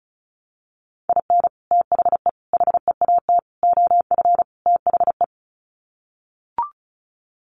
Morse code, heard as a keyed tone, about 34 wpm, 700 Hz. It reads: IN THE HEAT OF THE
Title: Sets of 5 Words Encore2 35wpm